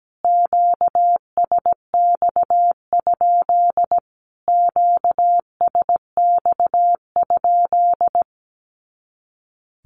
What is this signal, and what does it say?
Morse code, heard as a keyed tone, about 17 wpm, 700 Hz. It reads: QSX? QSX?
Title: Q-Codes Repeated 3x 17wpm